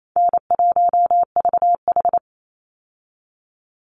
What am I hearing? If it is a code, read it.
Morse code, heard as a keyed tone, about 28 wpm, 700 Hz. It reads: N1VH
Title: Call Signs Repeated 2x - 28wpm